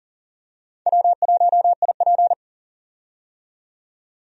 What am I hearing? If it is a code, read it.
Morse code, heard as a keyed tone, about 41 wpm, 690 Hz. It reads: W1IP
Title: USA Call Signs - 4 Characters - 40wpm